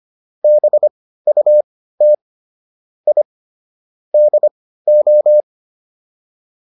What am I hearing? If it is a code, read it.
Morse code, heard as a keyed tone, about 25 wpm, spaced as for 15 wpm, 600 Hz. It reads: BUT I DO